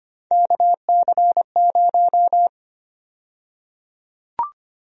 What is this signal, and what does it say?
Morse code, heard as a keyed tone, about 25 wpm, 700 Hz. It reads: KC0
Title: USA Call Sign Prefixes - Repeated 2x - 25wpm